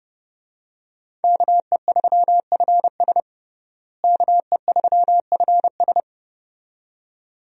Morse code, heard as a keyed tone, about 32 wpm, 700 Hz. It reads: KE3FS KE3FS
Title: USA Call Signs - 5 Characters - Repeated 2x - 30wpm